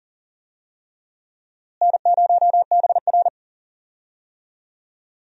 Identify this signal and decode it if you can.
Morse code, heard as a keyed tone, about 40 wpm, 700 Hz. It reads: N0BR
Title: USA Call Signs - 4 Characters - Repeated 3x - 40wpm